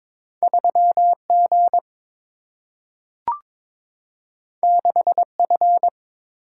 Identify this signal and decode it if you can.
Morse code, heard as a keyed tone, about 22 wpm, 700 Hz. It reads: 3 G 6 F
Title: Single-Letter Single-Number 22wpm